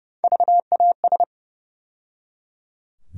Morse code, heard as a keyed tone, about 30 wpm, 700 Hz. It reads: VAS
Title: USA Call Sign Postfixes - 3 Letters 30wpm